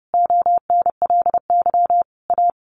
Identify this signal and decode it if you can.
Morse code, heard as a keyed tone, about 30 wpm, 700 Hz. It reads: ONLY A
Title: A Sound Of Thunder 30wpm